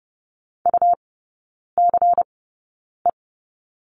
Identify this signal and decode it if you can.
Morse code, heard as a keyed tone, about 30 wpm, spaced as for 9 wpm, 700 Hz. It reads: UCE
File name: Lesson 18e - Letter m - Three Characters 30wpm